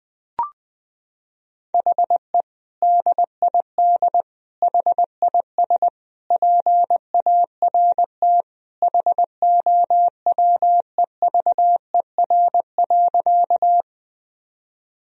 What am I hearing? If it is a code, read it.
Morse code, heard as a keyed tone, about 21 wpm, 700 Hz. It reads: HE DID HIS PART HOWEVER.
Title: Sentences from Top 1000 Words - Review Part 1 - 20wpm